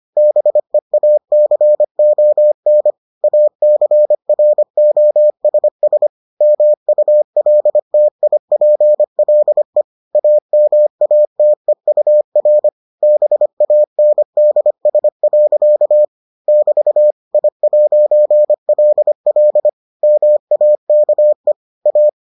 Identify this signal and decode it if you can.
Morse code, heard as a keyed tone, about 25 wpm, 600 Hz. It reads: BEACON ACROSS MULTIPLE AMATEUR BANDS. <BT> I'LL MAKE A